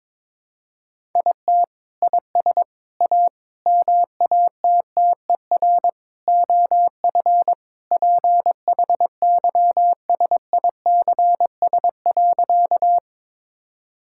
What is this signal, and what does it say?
Morse code, heard as a keyed tone, about 22 wpm, 700 Hz. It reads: IT IS A MATTER OF PHYSICS.
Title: Sentences from Top 1000 Words - Review Part 5 - 22wpm